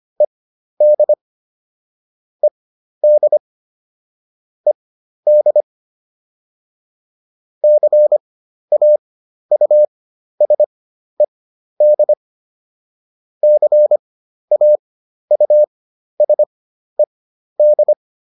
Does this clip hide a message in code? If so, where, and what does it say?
Yes: Morse code from 0.2 to 17.9 s, heard as a keyed tone, about 25 wpm, spaced as for 12 wpm, 600 Hz. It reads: ED ED ED CAUSED CAUSED